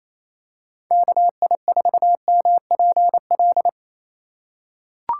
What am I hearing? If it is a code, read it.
Morse code, heard as a keyed tone, about 28 wpm, 700 Hz. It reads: KI4MPL
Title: USA Call Signs - 6 Characters - 3x Character Spacing - 28wpm